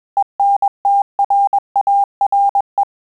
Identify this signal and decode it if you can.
Morse code, heard as a keyed tone, about 21 wpm, 800 Hz. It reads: ENTRARE
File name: Telegraphy-group / breve testo midi